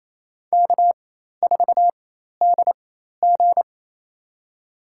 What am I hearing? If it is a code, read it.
Morse code, heard as a keyed tone, about 27 wpm, spaced as for 13 wpm, 700 Hz. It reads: K4DG
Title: USA Call Signs - 4x Character Spacing - 28wpm